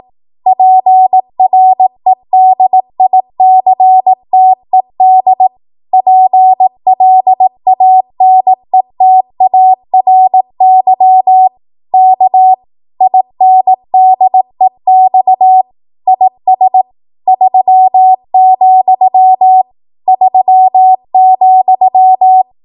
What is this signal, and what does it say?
Morse code, heard as a keyed tone, about 18 wpm, 750 Hz. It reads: PREDICTED PLANETARY K INDEX IS 3, 3,
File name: Morse / 18WPM